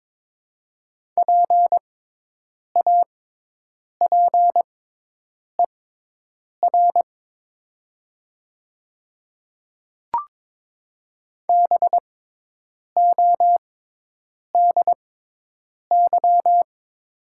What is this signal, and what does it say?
Morse code, heard as a keyed tone, about 22 wpm, spaced as for 8 wpm, 700 Hz. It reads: PAPER BODY
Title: 6x Character Spacing - Top 500 Words - 22wpm